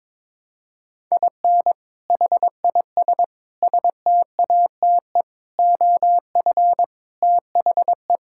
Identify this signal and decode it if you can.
Morse code, heard as a keyed tone, about 22 wpm, 700 Hz. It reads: IN HIS STATE OF THE
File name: practice / 22wpm / Sets of 7 Words - 22wpm